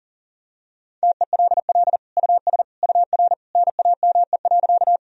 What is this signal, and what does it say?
Morse code, heard as a keyed tone, about 40 wpm, 700 Hz. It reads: TELL US UR NAME.
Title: Sentences from Top 200 Words and Abbreviations 40wpm